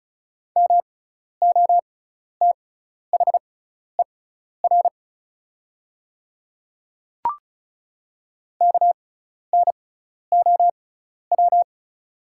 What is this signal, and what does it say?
Morse code, heard as a keyed tone, about 36 wpm, spaced as for 12 wpm, 690 Hz. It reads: MOTHER KNOW
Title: 6x Character Spacing - Top 200 Words - 35wpm